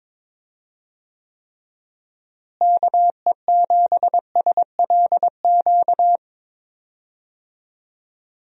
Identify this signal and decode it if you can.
Morse code, heard as a keyed tone, about 22 wpm, 700 Hz. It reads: KE7SLQ
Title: USA Call Signs - 6 Characters - Repeated 3x - 22wpm